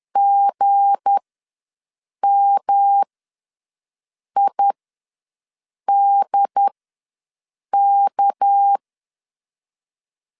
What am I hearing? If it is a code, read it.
Morse code, heard as a keyed tone, about 10 wpm, spaced as for 6 wpm, 800 Hz. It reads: GMIDK
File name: LEZIONI / 7 GF9Z / 7 G ws 10wpm 3843